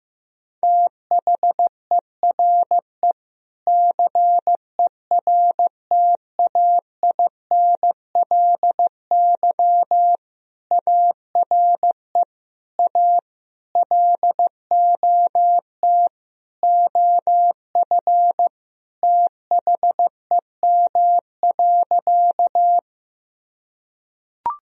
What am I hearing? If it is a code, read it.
Morse code, heard as a keyed tone, about 15 wpm, 700 Hz. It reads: THERE CERTAINLY ARE A LOT OF THEM.
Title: Sentences from Top 1000 Words - Review Part 3 - 15wpm